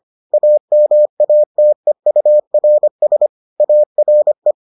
Morse code, heard as a keyed tone, about 25 wpm, 600 Hz. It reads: AMATEURS ARE